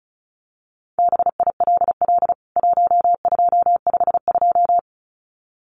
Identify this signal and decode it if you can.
Morse code, heard as a keyed tone, about 34 wpm, 700 Hz. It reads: BILL 1252
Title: CWT Exchange 35wpm